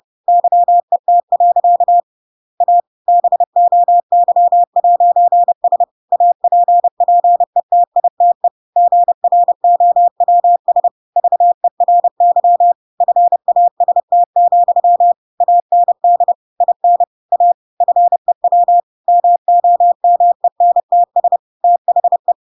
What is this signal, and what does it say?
Morse code, heard as a keyed tone, about 30 wpm, 700 Hz. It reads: YET. A BOY'S APPETITE GROWS VERY FAST, AND IN A FEW MOMENTS THE